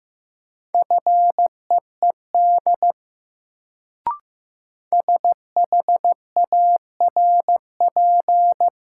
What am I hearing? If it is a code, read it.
Morse code, heard as a keyed tone, about 15 wpm, 700 Hz. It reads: FEED SHARP
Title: Sentences from Top 1600 Words - New Word Drill - 15wpm